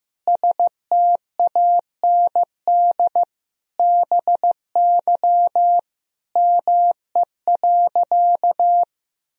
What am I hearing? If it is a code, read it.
Morse code, heard as a keyed tone, about 15 wpm, 700 Hz. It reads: STAND BY ME.
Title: Sentences from Top 1000 Words - Review Part 1 - 15wpm